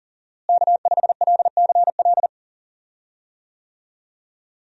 Morse code, heard as a keyed tone, about 40 wpm, 700 Hz. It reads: K5LCL
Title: USA Call Signs - 5 Characters - Repeated 3x - 40wpm